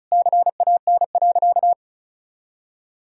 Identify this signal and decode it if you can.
Morse code, heard as a keyed tone, about 36 wpm, 700 Hz. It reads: CAN.
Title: Sentences from Top 500 Words and Abbreviations 35wpm